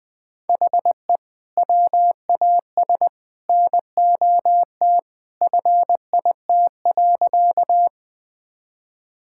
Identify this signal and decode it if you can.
Morse code, heard as a keyed tone, about 20 wpm, 700 Hz. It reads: HE WAS NOT FIT.
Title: Sentences from Top 1000 Words - Review Part 5 - 20wpm